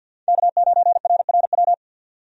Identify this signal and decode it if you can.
Morse code, heard as a keyed tone, about 51 wpm, 700 Hz. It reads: K9RRW
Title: USA Call Signs - 5 Characters - 50wpm